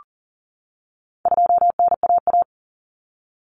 Morse code, heard as a keyed tone, about 40 wpm, 700 Hz. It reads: 2NAA
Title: Common Cut Numbers - 4-Digit 40wpm